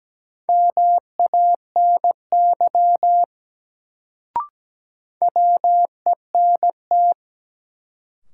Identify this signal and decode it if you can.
Morse code, heard as a keyed tone, about 17 wpm, 700 Hz. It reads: MANY WENT
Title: Sets of 2 Words - New Word Drill - 17wpm